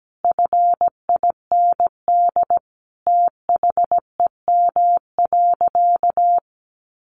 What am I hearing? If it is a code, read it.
Morse code, heard as a keyed tone, about 17 wpm, 700 Hz. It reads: FIND THEM.
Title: Sentences from Top 1500 Words 17wpm